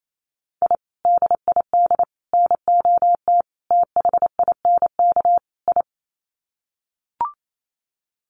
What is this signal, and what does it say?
Morse code, heard as a keyed tone, about 28 wpm, 700 Hz. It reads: I DID NOT THINK I
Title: Sets of 5 Words Encore 28wpm